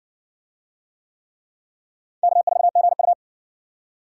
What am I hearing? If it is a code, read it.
Morse code, heard as a keyed tone, about 60 wpm, 690 Hz. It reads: K4GU